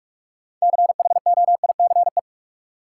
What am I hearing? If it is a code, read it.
Morse code, heard as a keyed tone, about 46 wpm, 700 Hz. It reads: CHOICE
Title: Sentences from Top 900 Words - New Word Drill - 45wpm